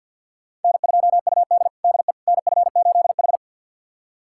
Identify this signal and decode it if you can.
Morse code, heard as a keyed tone, about 52 wpm, 700 Hz. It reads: N2UD DE NF8H
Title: USA Callsign DE Callsign - 4 Characters - 50wpm